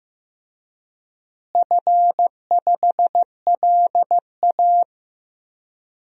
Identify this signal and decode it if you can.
Morse code, heard as a keyed tone, about 15 wpm, 700 Hz. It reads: F5LA
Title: DXCC-100 Call Signs - 4 Characters - Repeated 3x - 15wpm